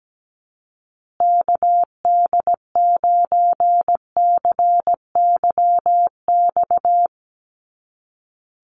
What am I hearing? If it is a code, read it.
Morse code, heard as a keyed tone, about 17 wpm, 700 Hz. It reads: KD9CYX
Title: USA Call Signs Repeated 3x 17wpm